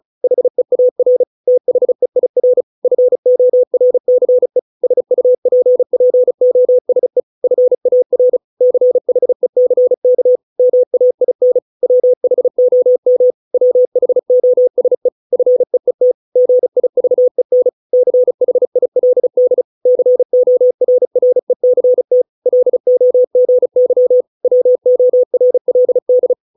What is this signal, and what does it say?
Morse code, heard as a keyed tone, about 34 wpm, 500 Hz. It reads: HEAR THEIR FORCE SUPPOSE FAR CHECK MAIN WHOM WHOSE FEET GIVEN CHILD CORRECT LOGY WORLD